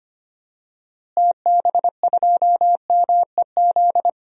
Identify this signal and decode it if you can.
Morse code, heard as a keyed tone, about 25 wpm, 700 Hz. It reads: TB2MEZ
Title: DXCC-100 Call Signs - 6 Characters - Repeated 3x - 25wpm